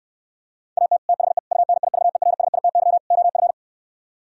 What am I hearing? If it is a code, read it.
Morse code, heard as a keyed tone, about 69 wpm, 700 Hz. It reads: AT THE UNIVERSITY OF